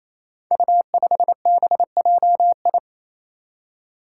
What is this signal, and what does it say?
Morse code, heard as a keyed tone, about 28 wpm, 700 Hz. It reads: U5BJI
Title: Five Challenging Letter-Number Combinations 28wpm